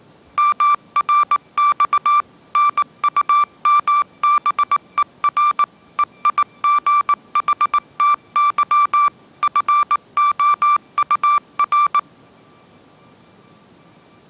Morse code, heard as a keyed tone, about 18 wpm, 1170 Hz. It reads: MRX NUMBER EIGHTY FOUR
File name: 20wpm